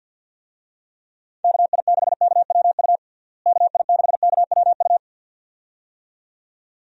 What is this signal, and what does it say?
Morse code, heard as a keyed tone, about 51 wpm, 700 Hz. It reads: KI6KWU KI6KWU
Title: USA Call Signs - 6 Characters - Repeated 2x - 50wpm